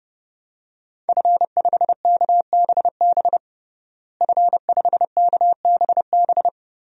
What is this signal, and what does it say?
Morse code, heard as a keyed tone, about 32 wpm, 700 Hz. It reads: F5KBB F5KBB
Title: DXCC-100 Call Signs - 5 Characters - Repeated 3x - 30wpm